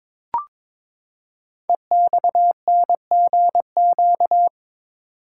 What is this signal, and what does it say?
Morse code, heard as a keyed tone, about 23 wpm, 700 Hz. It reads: EXNGQ
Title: Code Groups - 5 Letters - 22wpm